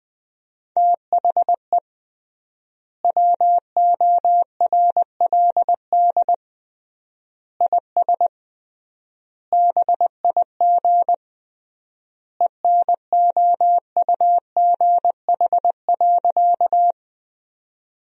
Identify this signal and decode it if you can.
Morse code, heard as a keyed tone, about 20 wpm, 700 Hz. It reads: THE WORLD IS BIG ENOUGH.
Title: 3x Word-Spacing - Sentences from Top 400 Words - 20wpm